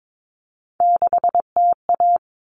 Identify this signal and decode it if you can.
Morse code, heard as a keyed tone, about 22 wpm, 700 Hz. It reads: 6TA
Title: All Cut Numbers - 3-Digit 22wpm